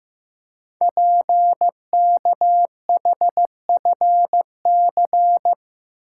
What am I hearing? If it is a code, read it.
Morse code, heard as a keyed tone, about 15 wpm, 700 Hz. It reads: PKHFC
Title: Code Groups - 5 Letters - 15wpm